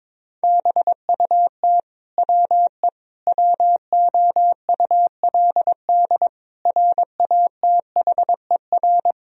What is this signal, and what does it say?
Morse code, heard as a keyed tone, about 22 wpm, 700 Hz. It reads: BUT WE WOULD RATHER
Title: Sentences from Top 1000 Words - Review Part 4 - 22wpm